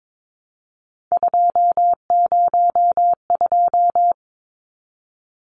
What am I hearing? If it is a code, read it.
Morse code, heard as a keyed tone, about 22 wpm, 700 Hz. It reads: 202
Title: Numbers - 3-Digit 22wpm